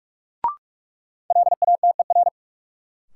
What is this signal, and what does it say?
Morse code, heard as a keyed tone, about 46 wpm, 700 Hz. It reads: LATER
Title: Sentences from Top 700 Words - New Word Drill - 45wpm